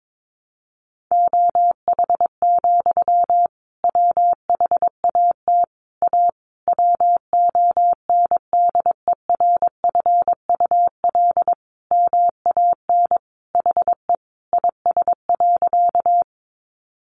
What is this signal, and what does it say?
Morse code, heard as a keyed tone, about 22 wpm, 700 Hz. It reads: OH, WHAT A WONDERFUL MAN HE IS.
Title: Sentences from Top 1100 Words 22wpm